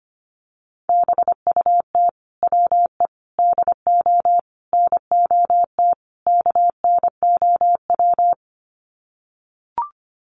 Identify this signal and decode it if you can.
Morse code, heard as a keyed tone, about 25 wpm, 700 Hz. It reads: BUT WE DO NOT KNOW
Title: Sets of 5 Words Encore2 25wpm